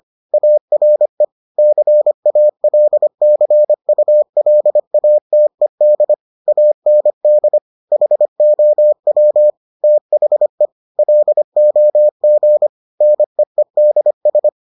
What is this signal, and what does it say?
Morse code, heard as a keyed tone, about 25 wpm, 600 Hz. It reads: ARE CALCULATED AND HOW THE LOG NEEDS